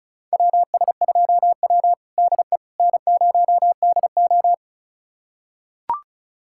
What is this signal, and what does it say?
Morse code, heard as a keyed tone, about 36 wpm, 700 Hz. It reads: WS2W DE N0DO
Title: USA Callsign DE Callsign - 4 Characters - 35wpm